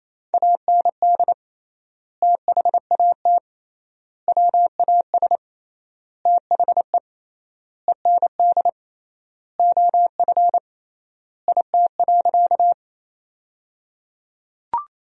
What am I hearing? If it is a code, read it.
Morse code, heard as a keyed tone, about 27 wpm, 700 Hz. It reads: AND THAT WAS THE END OF IT.
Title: 3x Word-Spacing - Sentences from Top 200 Words - 28wpm